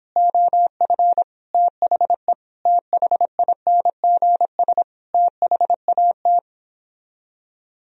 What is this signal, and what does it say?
Morse code, heard as a keyed tone, about 29 wpm, 700 Hz. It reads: OF THE THINGS THAT